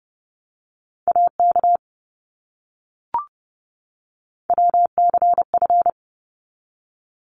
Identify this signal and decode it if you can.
Morse code, heard as a keyed tone, about 30 wpm, 700 Hz. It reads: AK WCF
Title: ARRL Sections 30wpm